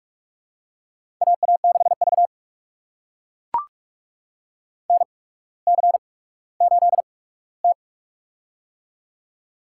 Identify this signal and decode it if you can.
Morse code, heard as a keyed tone, about 44 wpm, spaced as for 19 wpm, 700 Hz. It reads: AA6V NC8T
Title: USA Call Signs - 4 Characters - 8x Character Spacing - 45wpm